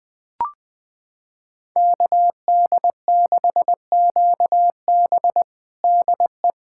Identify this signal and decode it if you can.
Morse code, heard as a keyed tone, about 20 wpm, 700 Hz. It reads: KD6QB DE
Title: USA Callsign DE Callsign - 5 Characters - 20wpm